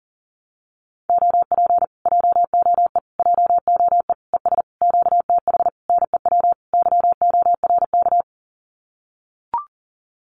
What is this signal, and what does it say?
Morse code, heard as a keyed tone, about 40 wpm, 700 Hz. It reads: OP JOE JOE ES QTH NEW YORK
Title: Common QSO Phrases Encore 40wpm